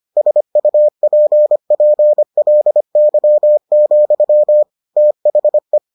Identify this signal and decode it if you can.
Morse code, heard as a keyed tone, about 25 wpm, 600 Hz. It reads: SUPPLY, THE